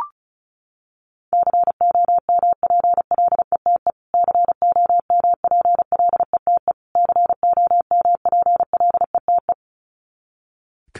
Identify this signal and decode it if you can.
Morse code, heard as a keyed tone, about 34 wpm, 700 Hz. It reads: COMPLETE COMPLETE COMPLETE
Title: Top 500 Words Repeated 35wpm